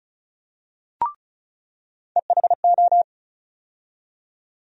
Morse code, heard as a keyed tone, about 36 wpm, 700 Hz. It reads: EHO
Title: USA Call Sign Postfixes - 3 Letters 35wpm